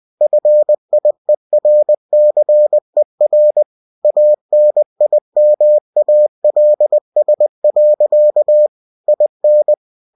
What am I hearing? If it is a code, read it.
Morse code, heard as a keyed tone, about 20 wpm, 600 Hz. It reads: FIERCER ANIMALS. IN
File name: cw-books / winnie-the-pooh / winnie-20wpm-page-0062